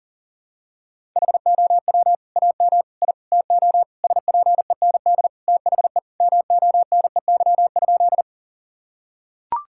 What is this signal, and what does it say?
Morse code, heard as a keyed tone, about 40 wpm, 700 Hz. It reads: HOW AM I TO SPEND THE MONEY?
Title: Sentences from Top 1000 Words - Review Part 5 - 40wpm